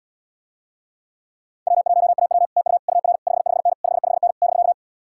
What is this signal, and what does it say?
Morse code, heard as a keyed tone, about 78 wpm, 700 Hz. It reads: W1AW TU UR 55N 55N <BK>